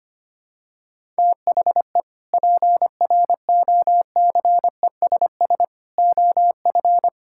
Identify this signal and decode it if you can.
Morse code, heard as a keyed tone, about 25 wpm, 700 Hz. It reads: THE PROCESS OF
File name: practice / 25wpm / Sentences with 3-Word Phrases - 25wpm